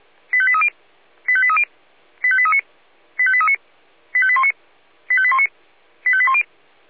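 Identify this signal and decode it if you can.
Five-tone selective call (ZVEI), heard as quick runs of five tones, tones of about 70 ms. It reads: 85830 85830 85839 85839 85819 85819 85810
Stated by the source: ZVEI five-tone call 85855 received on an SDR.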